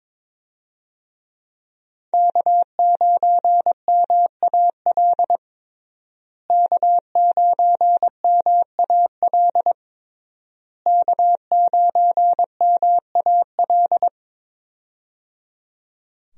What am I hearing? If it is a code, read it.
Morse code, heard as a keyed tone, about 22 wpm, 700 Hz. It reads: K9MAL K9MAL K9MAL
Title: USA Call Signs - 5 Characters - Repeated 3x - 22wpm